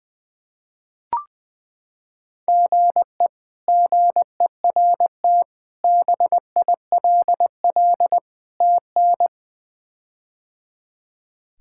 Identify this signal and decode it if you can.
Morse code, heard as a keyed tone, about 20 wpm, 700 Hz. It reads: GE GERT BILL TN
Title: K1USN SST QSO Practice - 20wpm